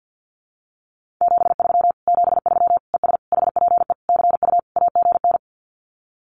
Morse code, heard as a keyed tone, about 50 wpm, 700 Hz. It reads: 73 73 ES HPE CU AGN
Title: Common QSO Phrases Encore 50wpm